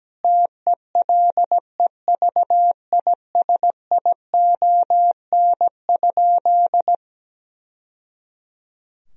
Morse code, heard as a keyed tone, about 17 wpm, 700 Hz. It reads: TELEVISION?
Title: Sentences from Top 1000 Words - Review Part 4 - 17wpm